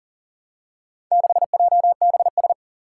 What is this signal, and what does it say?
Morse code, heard as a keyed tone, about 40 wpm, 700 Hz. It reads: 6JBS
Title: Four Challenging Letter-Number Combinations 40wpm